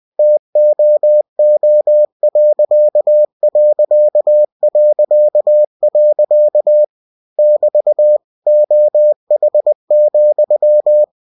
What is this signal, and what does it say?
Morse code, heard as a keyed tone, about 20 wpm, 600 Hz. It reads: TOO.... <BT> OH,
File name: cw-books / winnie-the-pooh / winnie-20wpm-page-0066